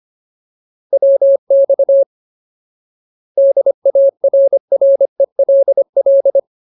Morse code, heard as a keyed tone, about 25 wpm, 550 Hz. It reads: WX DARRELL
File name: Ultimate Word List / 25WPM Ultimate